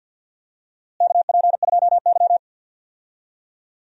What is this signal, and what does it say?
Morse code, heard as a keyed tone, about 52 wpm, 700 Hz. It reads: KP2Y
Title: DXCC-100 Call Signs - 4 Characters - Repeated 2x - 50wpm